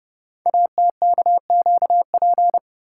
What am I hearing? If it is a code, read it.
Morse code, heard as a keyed tone, about 30 wpm, 700 Hz. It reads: ATKQP
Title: Code Groups - 5 Letters - 30wpm